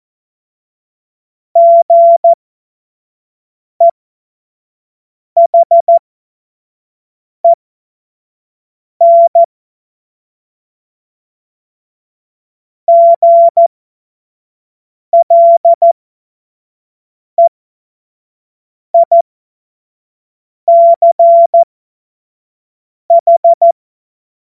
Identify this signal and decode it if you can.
Morse code, heard as a keyed tone, about 14 wpm, spaced as for 5 wpm, 680 Hz. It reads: GEHEN GLEICH
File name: Morsekurs / Witze / Witz 06. Dezember 2022 25 F70